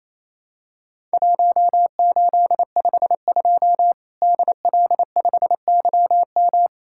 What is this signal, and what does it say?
Morse code, heard as a keyed tone, about 28 wpm, 700 Hz. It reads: 1852 DL5YM
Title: Worked All Europe DX Contest - QTC Exchange - 28wpm